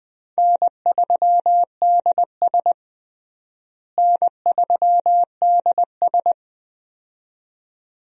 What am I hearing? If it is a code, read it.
Morse code, heard as a keyed tone, about 20 wpm, 700 Hz. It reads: N3DS N3DS
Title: USA Call Signs - 4 Characters - Repeated 2x - 20wpm